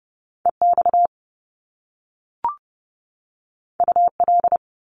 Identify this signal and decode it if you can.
Morse code, heard as a keyed tone, about 30 wpm, 700 Hz. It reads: EX UL
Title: Top Two Letter Combinations in Words 30wpm